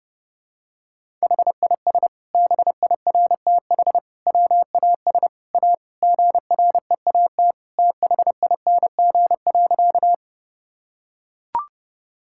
Morse code, heard as a keyed tone, about 30 wpm, 700 Hz. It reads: HIS BIRTH WAS A GREAT THING.
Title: Sentences from Top 1000 Words - Review Part 5 - 30wpm